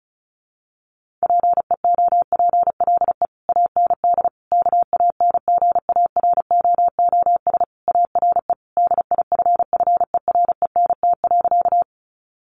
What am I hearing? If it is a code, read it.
Morse code, heard as a keyed tone, about 34 wpm, 700 Hz. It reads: PEOPLE AND KANGAROOS ARE DIFFERENT.
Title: Easy Phrases 35wpm